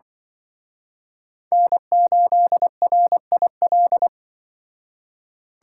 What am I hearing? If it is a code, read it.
Morse code, heard as a keyed tone, about 25 wpm, 700 Hz. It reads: N8RIL
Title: Speed Racing - USA 1x3 Call Signs 20wpm